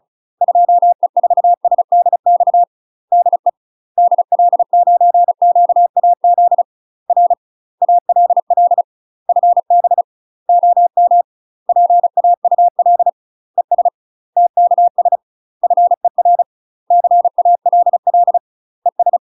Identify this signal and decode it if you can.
Morse code, heard as a keyed tone, about 36 wpm, 700 Hz. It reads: 2E4SDX DE DL9QAZ R ALL FB OM PAUL ES TKS FER CALL ES
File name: UK MASTER QSOs TRAINING / 35WPM / UK MASTER QSOs 0009